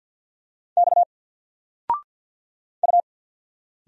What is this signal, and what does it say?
Morse code, heard as a keyed tone, about 54 wpm, 700 Hz. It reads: X U
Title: Single Letter-Number Alt 50wpm